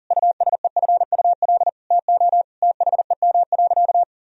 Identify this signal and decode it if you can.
Morse code, heard as a keyed tone, about 37 wpm, 700 Hz. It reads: USEFUL TO THEM.
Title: Sentences from Top 1000 Words - Review Part 4 - 40wpm